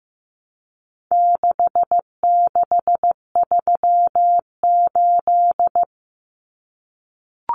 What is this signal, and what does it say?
Morse code, heard as a keyed tone, about 15 wpm, 700 Hz. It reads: 6638
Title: Numbers - 4-Digit 15wpm